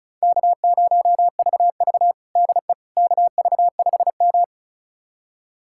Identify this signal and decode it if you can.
Morse code, heard as a keyed tone, about 36 wpm, 700 Hz. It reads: K0VV DE KV5M
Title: USA Callsign DE Callsign - 4 Characters - 35wpm